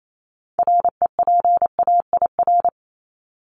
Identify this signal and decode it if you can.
Morse code, heard as a keyed tone, about 28 wpm, 700 Hz. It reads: REPAIR
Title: Top 1501-1750 Words 28wpm